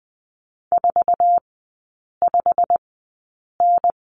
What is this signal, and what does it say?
Morse code, heard as a keyed tone, about 20 wpm, spaced as for 8 wpm, 700 Hz. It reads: 45N
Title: 100 Lesson 17e - Letter u - Three Characters 20wpm